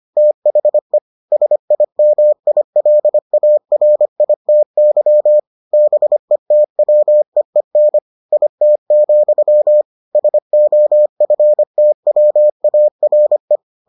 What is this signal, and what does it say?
Morse code, heard as a keyed tone, about 25 wpm, 600 Hz. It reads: THE SIMILARITY BETWEEN IT, SOFTWARE